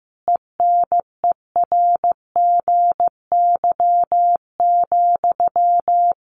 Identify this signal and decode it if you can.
Morse code, heard as a keyed tone, about 15 wpm, 700 Hz. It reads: ENERGY,
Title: Sentences from Top 900 Words 15wpm